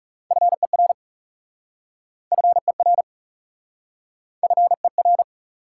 Morse code, heard as a keyed tone, about 36 wpm, 700 Hz. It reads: FER FER FER
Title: Speed-Racing Top 90 QSO Elements 30wpm